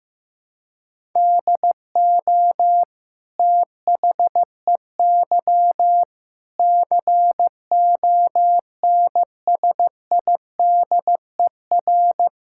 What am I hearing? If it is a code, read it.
Morse code, heard as a keyed tone, about 15 wpm, 700 Hz. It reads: DO THEY CONSIDER
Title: Sentences from Top 1000 Words - Review Part 6 - 15wpm